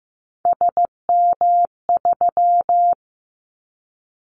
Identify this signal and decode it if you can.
Morse code, heard as a keyed tone, about 15 wpm, 700 Hz. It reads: SM3
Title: Callsign Prefixes for Top 25 DXCC - 15wpm